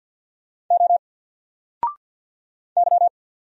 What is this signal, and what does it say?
Morse code, heard as a keyed tone, about 48 wpm, 700 Hz. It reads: O Y
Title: Single Letter-Number Alt 50wpm